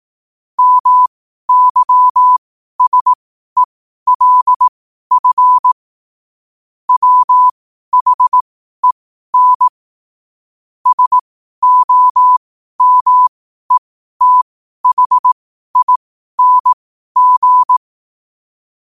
Morse code, heard as a keyed tone, about 18 wpm, spaced as for 12 wpm, 1000 Hz. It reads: MYSELF WHEN SOMETHING
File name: qotdfiles / 12wpm